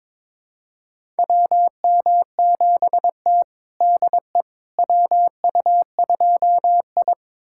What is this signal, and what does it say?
Morse code, heard as a keyed tone, about 22 wpm, 700 Hz. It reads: WM7T DE WU2I
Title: USA Callsign DE Callsign - 4 Characters - 22wpm